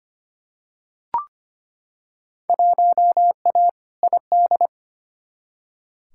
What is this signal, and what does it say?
Morse code, heard as a keyed tone, about 25 wpm, 700 Hz. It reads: 1A ID
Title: Field Day Exchanges - 25wpm